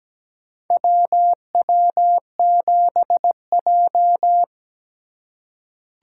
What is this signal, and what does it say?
Morse code, heard as a keyed tone, about 17 wpm, 700 Hz. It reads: WW7J
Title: USA Call Signs - 4 Characters - 4x Character Spacing - 17wpm